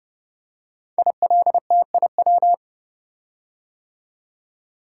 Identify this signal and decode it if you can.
Morse code, heard as a keyed tone, about 30 wpm, 700 Hz. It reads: ILTIW
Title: Code Groups - 5 Letters - 30wpm